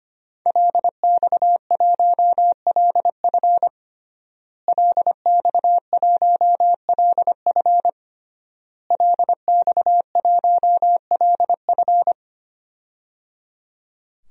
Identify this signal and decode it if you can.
Morse code, heard as a keyed tone, about 25 wpm, 700 Hz. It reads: LX1LF LX1LF LX1LF
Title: DXCC-100 Call Signs - 5 Characters - Repeated 3x - 25wpm